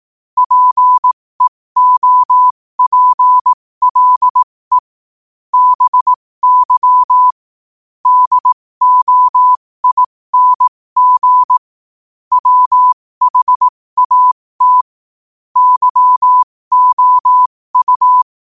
Morse code, heard as a keyed tone, about 18 wpm, spaced as for 15 wpm, 1000 Hz. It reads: PEOPLE BY DOING WHAT YOU
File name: qotdfiles / 15wpm